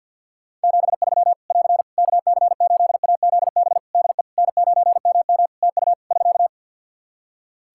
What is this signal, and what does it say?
Morse code, heard as a keyed tone, about 50 wpm, 700 Hz. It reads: 73 <AR> KC8AZB DE N9MM TU <SK>
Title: K1USN SST QSO Practice 50wpm